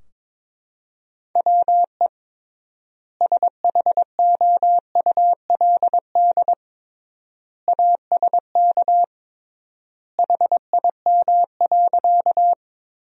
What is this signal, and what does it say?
Morse code, heard as a keyed tone, about 22 wpm, 700 Hz. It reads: WE SHOULD ASK HIM.
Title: 3x Word-Spacing - Sentences from Top 300 Words - 22wpm